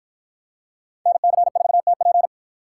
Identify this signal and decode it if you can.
Morse code, heard as a keyed tone, about 48 wpm, 700 Hz. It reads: NC4TP